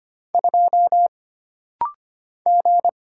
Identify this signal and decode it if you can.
Morse code, heard as a keyed tone, about 25 wpm, 700 Hz. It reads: 2 G
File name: Single Character (Letter:Number:Punctuation:Prosign) - Rapid-Fire - 25wpm